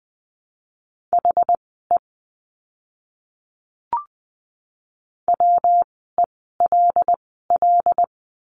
Morse code, heard as a keyed tone, about 20 wpm, spaced as for 15 wpm, 700 Hz. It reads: HE WELL
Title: 192 Lesson 31f - Letter v - Words 20wpm